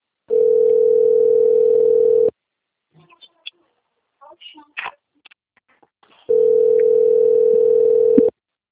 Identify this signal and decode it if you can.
Ringback tone (North America), 440 + 480 Hz, with speech.
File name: rg-1600-unknown- - - .64529